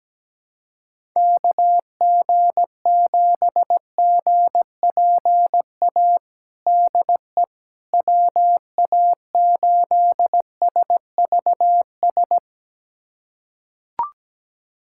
Morse code, heard as a keyed tone, about 17 wpm, 700 Hz. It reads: KG7GPA DE WA8SVS
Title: USA Callsign DE Callsign 17wpm